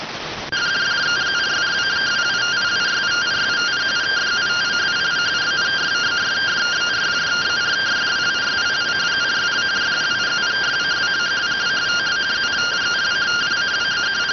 Two-tone teleprinter signal, about 170 Hz shift.